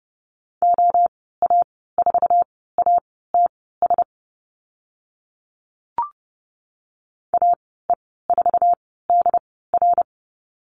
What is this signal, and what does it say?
Morse code, heard as a keyed tone, about 30 wpm, spaced as for 17 wpm, 700 Hz. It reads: OA4ATS AE4DR
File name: Lesson 15g - Number 5 - Call Signs 30wpm